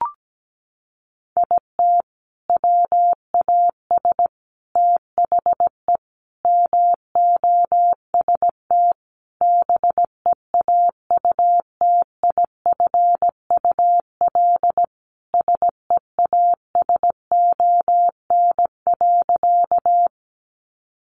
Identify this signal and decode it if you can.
Morse code, heard as a keyed tone, about 17 wpm, 700 Hz. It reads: IT WAS THE MOST BEAUTIFUL SEASON.
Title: Sentences from Top 1400 Words 17wpm